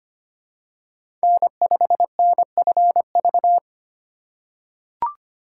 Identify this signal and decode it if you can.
Morse code, heard as a keyed tone, about 25 wpm, 700 Hz. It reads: N5NFV
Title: USA Call Signs - 5 Characters - 4x Character Spacing - 25wpm